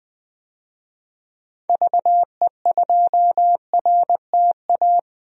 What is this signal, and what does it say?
Morse code, heard as a keyed tone, about 20 wpm, 700 Hz. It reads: VE2RTA
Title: DXCC-100 Call Signs - 6 Characters - 20wpm